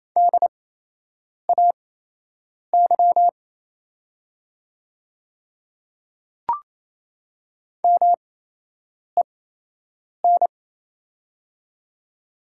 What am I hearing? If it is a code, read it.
Morse code, heard as a keyed tone, about 28 wpm, spaced as for 8 wpm, 700 Hz. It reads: DAY MEN
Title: 8x Character Spacing - Top 300 Words - 28wpm